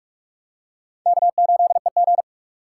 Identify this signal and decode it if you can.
Morse code, heard as a keyed tone, about 44 wpm, 690 Hz. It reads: K8EG
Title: USA Call Signs - 4 Characters - 2x Character Spacing - 45wpm